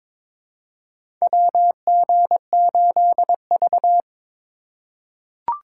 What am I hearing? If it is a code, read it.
Morse code, heard as a keyed tone, about 22 wpm, 700 Hz. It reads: WG8V
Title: USA Call Signs - 2x Character Spacing - 22wpm